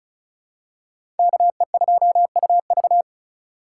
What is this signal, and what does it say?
Morse code, heard as a keyed tone, about 34 wpm, 700 Hz. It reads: KE2UV
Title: USA Call Signs - 5 Characters - 3x Character Spacing - 35wpm